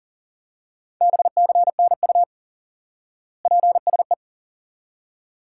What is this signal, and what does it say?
Morse code, heard as a keyed tone, about 37 wpm, 700 Hz. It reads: BCNU PSE
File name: IQE Course 3 - Lesson 89 - YAGI - Two Elements - 40wpm